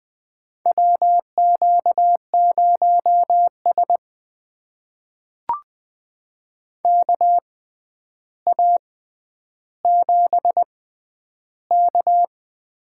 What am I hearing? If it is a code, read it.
Morse code, heard as a keyed tone, about 20 wpm, spaced as for 10 wpm, 700 Hz. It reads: WQ0S KA7K
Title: USA Call Signs - 4 Characters - 6x Character Spacing - 20wpm